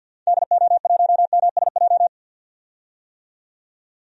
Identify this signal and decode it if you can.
Morse code, heard as a keyed tone, about 47 wpm, 700 Hz. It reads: DO1MSJ
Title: DXCC-100 Call Signs - 6 Characters - 50wpm